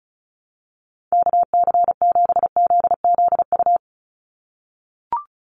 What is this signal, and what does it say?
Morse code, heard as a keyed tone, about 36 wpm, 700 Hz. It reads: KC7ZZU
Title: USA Call Signs 35wpm